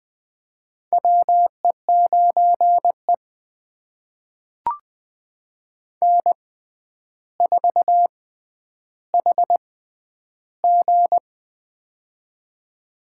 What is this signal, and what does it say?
Morse code, heard as a keyed tone, about 20 wpm, spaced as for 10 wpm, 700 Hz. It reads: WE9E N4HG
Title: USA Call Signs - 4 Characters - 6x Character Spacing - 20wpm